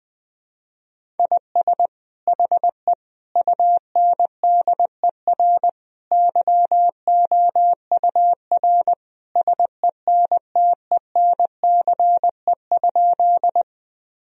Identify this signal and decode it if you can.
Morse code, heard as a keyed tone, about 20 wpm, 700 Hz. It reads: IS HE UNDER YOUR SENTENCE?
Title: Sentences from Top 1000 Words - Review Part 1 - 20wpm